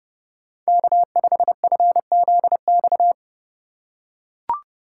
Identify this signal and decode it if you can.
Morse code, heard as a keyed tone, about 29 wpm, 700 Hz. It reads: K5FZX